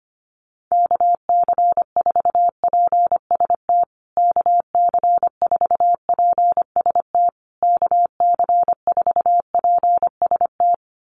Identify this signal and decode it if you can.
Morse code, heard as a keyed tone, about 25 wpm, 700 Hz. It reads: KC4PST KC4PST KC4PST
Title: USA Call Signs Repeated 3x 25wpm